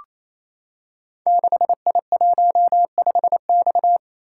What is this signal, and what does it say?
Morse code, heard as a keyed tone, about 28 wpm, 700 Hz. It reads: 6I15X
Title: Code Groups - 5 Letter:Numbers - 28wpm